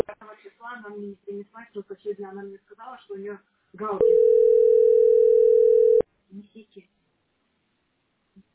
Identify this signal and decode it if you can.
Ringback tone (North America), 440 + 480 Hz, with speech.
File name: 7ed2690e-fafd-49d0-921d